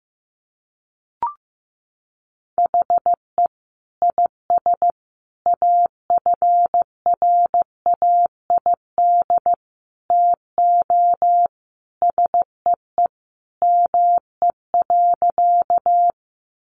Morse code, heard as a keyed tone, about 15 wpm, 700 Hz. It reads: HE IS AFRAID TO SEE ME.
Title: Sentences from Top 1500 Words 15wpm